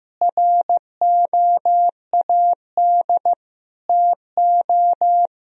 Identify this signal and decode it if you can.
Morse code, heard as a keyed tone, about 15 wpm, 700 Hz. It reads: ROAD TO
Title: Sentences from Top 400 Words and Abbreviations 15wpm